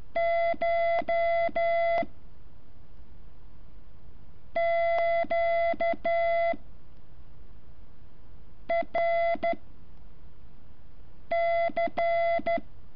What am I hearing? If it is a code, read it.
Morse code, heard as a keyed tone, about 13 wpm, 680 Hz. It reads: CH Q R C
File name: curso / ejercicio 14